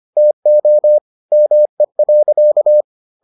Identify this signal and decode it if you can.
Morse code, heard as a keyed tone, about 25 wpm, 600 Hz. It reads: TO ME.